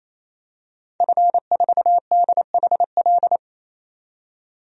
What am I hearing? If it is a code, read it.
Morse code, heard as a keyed tone, about 28 wpm, 700 Hz. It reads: F4DHL
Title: DXCC-100 Call Signs - 5 Characters - 28wpm